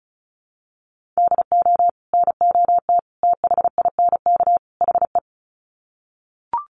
Morse code, heard as a keyed tone, about 34 wpm, 700 Hz. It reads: DO NOT THINK HE
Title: Sets of 4 Words 35wpm